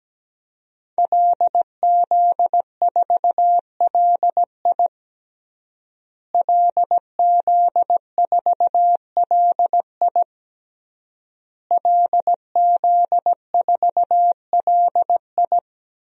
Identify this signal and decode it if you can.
Morse code, heard as a keyed tone, about 17 wpm, 700 Hz. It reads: LZ4LI LZ4LI LZ4LI
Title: DXCC-100 Call Signs - 5 Characters - Repeated 3x - 17wpm